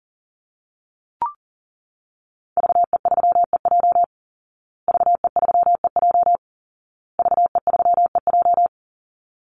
Morse code, heard as a keyed tone, about 40 wpm, 700 Hz. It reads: VE3EJ VE3EJ VE3EJ
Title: Call Signs Repeated 2x - 40wpm